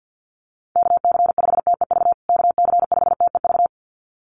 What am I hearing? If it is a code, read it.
Morse code, heard as a keyed tone, about 50 wpm, 700 Hz. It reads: KC5TEV KC5TEV
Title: USA Call Signs Repeated 2x 50wpm